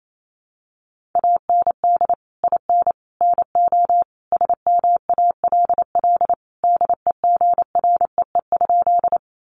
Morse code, heard as a keyed tone, about 28 wpm, 700 Hz. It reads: AND IN NO SMALL DEGREE?
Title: Sentences from Top 1500 Words 28wpm